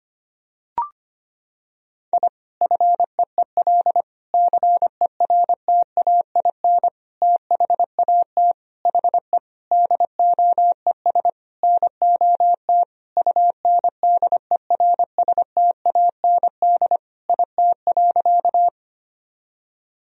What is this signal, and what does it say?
Morse code, heard as a keyed tone, about 25 wpm, 700 Hz. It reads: I FEEL CERTAIN THAT HE DOES NOT UNDERSTAND IT.
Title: Sentences from Top 1000 Words - Review Part 2 - 25wpm